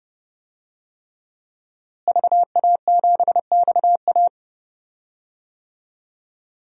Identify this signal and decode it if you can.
Morse code, heard as a keyed tone, about 30 wpm, 700 Hz. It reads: VA7XA
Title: DXCC-100 Call Signs - 5 Characters - Repeated 3x - 30wpm